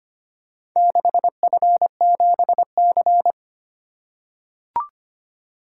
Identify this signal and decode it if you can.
Morse code, heard as a keyed tone, about 25 wpm, 700 Hz. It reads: 6F7C
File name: Four Letter-Number Combinations - 25wpm